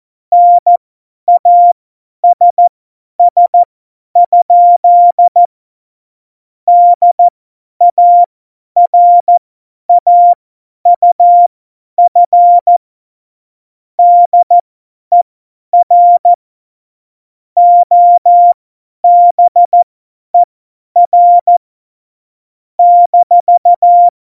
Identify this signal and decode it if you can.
Morse code, heard as a keyed tone, about 14 wpm, spaced as for 10 wpm, 700 Hz. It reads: NASS? DARAUF DER OBER -
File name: Morsekurs / Witze / Witz 10.Dezember 2019, 50 F70